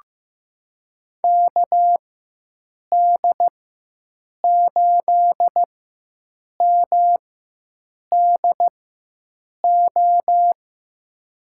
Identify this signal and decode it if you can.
Morse code, heard as a keyed tone, about 15 wpm, spaced as for 7 wpm, 700 Hz. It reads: KD8MDO
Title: USA Call Signs - 6 Characters - 4x Character Spacing - 15wpm